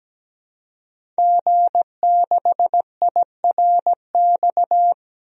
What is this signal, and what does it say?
Morse code, heard as a keyed tone, about 17 wpm, 700 Hz. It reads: G6IRX
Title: DXCC-100 Call Signs - 5 Characters - 17wpm